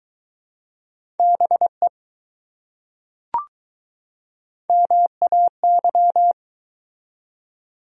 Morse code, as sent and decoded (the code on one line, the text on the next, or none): -... / -- .- -.--
BE MAY